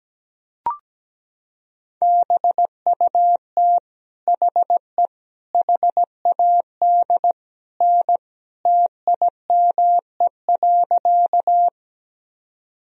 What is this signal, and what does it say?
Morse code, heard as a keyed tone, about 17 wpm, 700 Hz. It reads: BUT HE HAD N TIME.
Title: Sentences from Top 100 Words and Abbreviations 17wpm